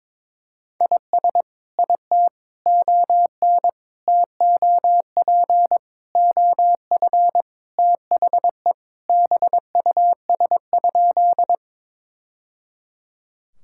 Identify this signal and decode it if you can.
Morse code, heard as a keyed tone, about 21 wpm, 700 Hz. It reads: IS IT ON TOP OF THE BUS?
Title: Sentences from Top 1000 Words - Review Part 6 - 22wpm